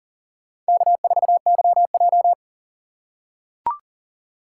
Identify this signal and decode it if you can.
Morse code, heard as a keyed tone, about 40 wpm, 700 Hz. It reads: K4YJ
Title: USA Call Signs - 4 Characters - 6x Character Spacing - 40wpm